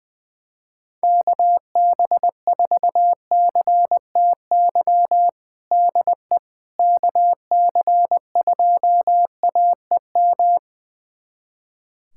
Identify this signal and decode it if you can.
Morse code, heard as a keyed tone, about 20 wpm, 700 Hz. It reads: KB4CTY DE KC2AEM
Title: USA Callsign DE Callsign 20wpm